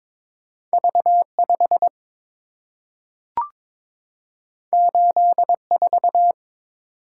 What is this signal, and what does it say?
Morse code, heard as a keyed tone, about 22 wpm, 700 Hz. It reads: V5 84
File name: practice / 22wpm / Two Letter-Number Combinations - 22wpm